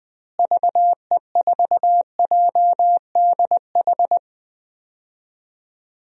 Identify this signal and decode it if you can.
Morse code, heard as a keyed tone, about 21 wpm, 700 Hz. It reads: VE4JDH
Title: DXCC-100 Call Signs - 6 Characters - Repeated 2x - 20wpm